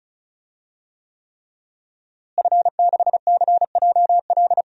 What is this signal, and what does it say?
Morse code, heard as a keyed tone, about 36 wpm, 700 Hz. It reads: F6CJL
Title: DXCC-100 Call Signs - 5 Characters - Repeated 3x - 35wpm